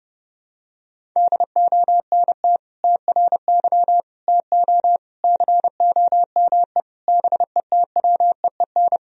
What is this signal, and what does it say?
Morse code, heard as a keyed tone, about 31 wpm, 700 Hz. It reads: DONT TRY TO COME BETWEEN
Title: Sentences from Top 1000 Words - Review Part 1 - 30wpm